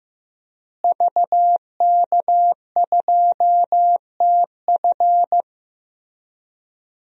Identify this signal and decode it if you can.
Morse code, heard as a keyed tone, about 15 wpm, 700 Hz. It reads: VK2TF
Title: DXCC-100 Call Signs - 5 Characters - Repeated 3x - 15wpm